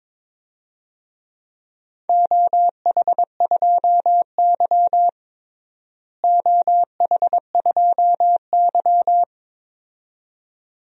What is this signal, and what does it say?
Morse code, heard as a keyed tone, about 22 wpm, 700 Hz. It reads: OH2Y OH2Y
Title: DXCC-100 Call Signs - 4 Characters - Repeated 2x - 22wpm